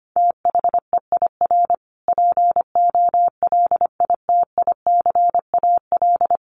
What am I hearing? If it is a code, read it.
Morse code, heard as a keyed tone, about 25 wpm, 700 Hz. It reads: THEIR POLITICAL
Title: Sentences from Top 1300 Words 25wpm